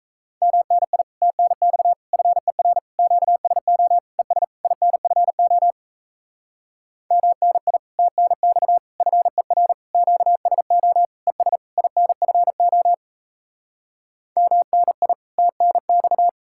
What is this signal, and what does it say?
Morse code, heard as a keyed tone, about 38 wpm, 700 Hz. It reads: MNI TNX FER QSO ES INFO MNI TNX FER QSO ES INFO MNI TNX